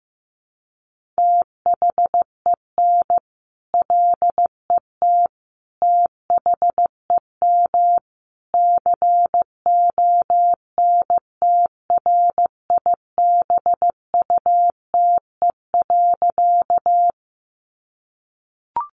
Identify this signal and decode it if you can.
Morse code, heard as a keyed tone, about 14 wpm, 700 Hz. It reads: THEN LET THEM CONTRIBUTE.
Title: Sentences from Top 1400 Words 15wpm